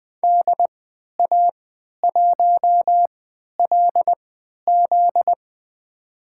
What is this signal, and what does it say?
Morse code, heard as a keyed tone, about 20 wpm, spaced as for 11 wpm, 700 Hz. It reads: DA1LZ
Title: USA Call Signs - 5 Characters - 3x Character Spacing - 20wpm